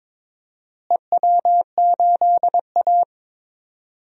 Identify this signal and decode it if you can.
Morse code, heard as a keyed tone, about 22 wpm, 700 Hz. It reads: EW8A
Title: DXCC-100 Call Signs - 4 Characters - Repeated 2x - 22wpm